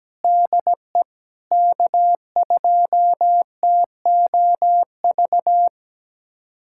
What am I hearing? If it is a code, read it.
Morse code, heard as a keyed tone, about 17 wpm, 700 Hz. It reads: DE K2TOV
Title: USA Callsign DE Callsign 17wpm